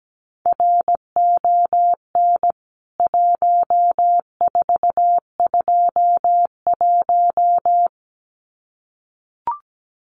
Morse code, heard as a keyed tone, about 17 wpm, 700 Hz. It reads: RON 1421
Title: CWT Exchange 17wpm